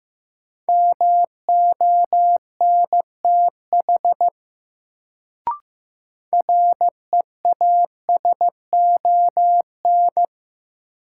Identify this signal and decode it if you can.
Morse code, heard as a keyed tone, about 15 wpm, 700 Hz. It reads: MONTH REASON
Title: Sets of 5 Words Encore - New Word Drill - 15wpm